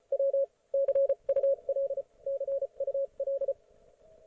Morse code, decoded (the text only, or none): W CUL CUL